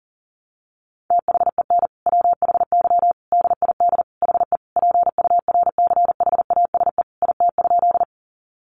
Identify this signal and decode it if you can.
Morse code, heard as a keyed tone, about 40 wpm, 700 Hz. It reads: THEN WHY DID HE PURCHASE IT?
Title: Sentences from Top 800 Words 40wpm